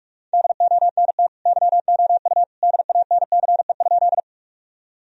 Morse code, heard as a keyed tone, about 48 wpm, 700 Hz. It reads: DONT YOU DANCE?
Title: Sentences from Top 1000 Words - Review Part 4 - 45wpm